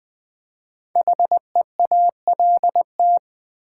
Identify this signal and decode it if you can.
Morse code, heard as a keyed tone, about 20 wpm, 700 Hz. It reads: HEALT
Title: Build A Word - 5 Letters At A Time - 10-Letter Words - 20wpm